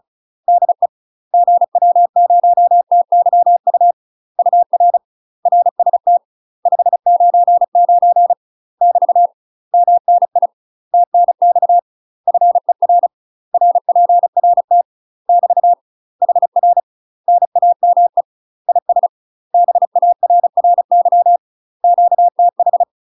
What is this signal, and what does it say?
Morse code, heard as a keyed tone, about 36 wpm, 700 Hz. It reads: DE GW0TYU UR RST 599 <BT> MNI TNX FER RPRT <BT> HR NAME IS BARRY QTH
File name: UK MASTER QSOs TRAINING / 35WPM / UK MASTER QSOs 0002